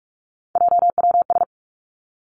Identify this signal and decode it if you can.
Morse code, heard as a keyed tone, about 46 wpm, 700 Hz. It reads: JWS
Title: Three Challenging Letter-Number Combinations 45wpm